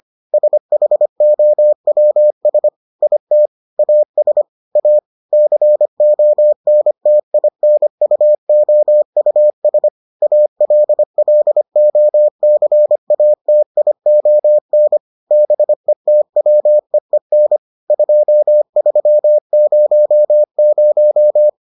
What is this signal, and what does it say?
Morse code, heard as a keyed tone, about 25 wpm, 600 Hz. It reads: SHOWS IT AS A CONTINUOUS ALLOCATION BETWEEN 2300